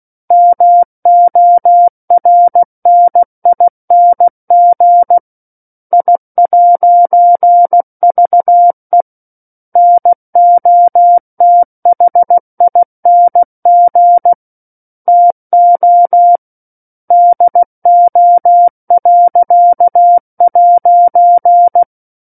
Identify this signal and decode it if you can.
Morse code, heard as a keyed tone, about 16 wpm, 700 Hz. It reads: MORNING I'VE NOTHING TO DO.'
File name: 03-Chapter-03 16 WPM